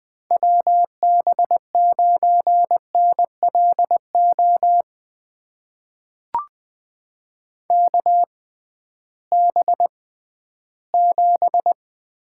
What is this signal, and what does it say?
Morse code, heard as a keyed tone, about 20 wpm, spaced as for 13 wpm, 700 Hz. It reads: WB9NLO KB7
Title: USA Call Signs - 6 Characters - 6x Character Spacing - 20wpm